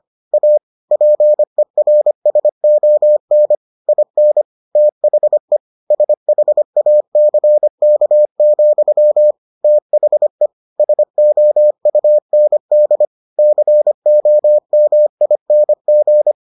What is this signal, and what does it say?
Morse code, heard as a keyed tone, about 25 wpm, 600 Hz. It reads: A PERSON IN THE SHACK, THE SOUND COMING